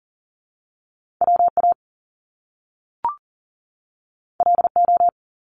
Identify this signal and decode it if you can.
Morse code, heard as a keyed tone, about 38 wpm, 700 Hz. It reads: WA LO
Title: Top Two Letter Combinations in Words 40wpm